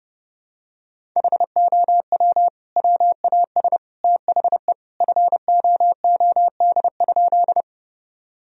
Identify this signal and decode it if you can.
Morse code, heard as a keyed tone, about 32 wpm, 700 Hz. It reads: HOW WAS THE FOOD?
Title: Sentences from Top 1000 Words - Review Part 1 - 30wpm